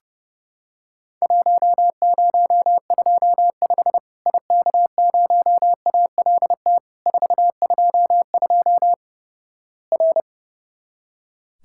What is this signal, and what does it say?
Morse code, heard as a keyed tone, about 30 wpm, 700 Hz. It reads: 1025 IK0ALT 422 R
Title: Worked All Europe DX Contest - QTC Exchange - 30wpm